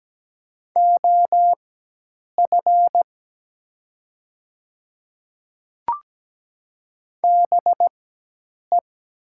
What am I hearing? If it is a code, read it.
Morse code, heard as a keyed tone, about 17 wpm, spaced as for 8 wpm, 700 Hz. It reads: OF BE
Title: 4x Character Spacing - Top 100 Words - 17wpm